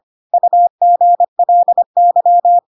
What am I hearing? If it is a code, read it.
Morse code, heard as a keyed tone, about 25 wpm, 700 Hz. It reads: UGLY
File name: Pinocchio / Pinocchio ch4 25 25 ik1hge